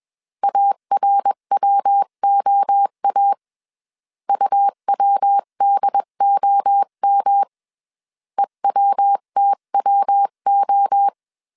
Morse code, heard as a keyed tone, about 21 wpm, 800 Hz. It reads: ARWOA UWDOM EWTWO
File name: 20wpm